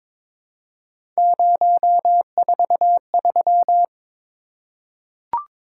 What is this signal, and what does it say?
Morse code, heard as a keyed tone, about 23 wpm, 700 Hz. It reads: 043
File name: practice / 22wpm / Three Letter-Number Combinations - 22wpm